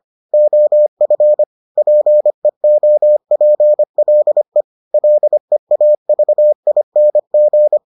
Morse code, heard as a keyed tone, about 25 wpm, 600 Hz. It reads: OF PEOPLE LEAVING